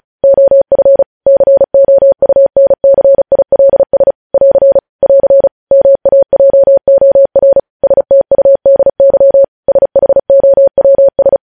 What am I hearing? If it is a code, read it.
Morse code, heard as a keyed tone, about 36 wpm, 550 Hz. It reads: OF COUNCILS <AR> <AR> MAJOR STUDY SHOWS
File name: podcast / News headlines-1x-35x35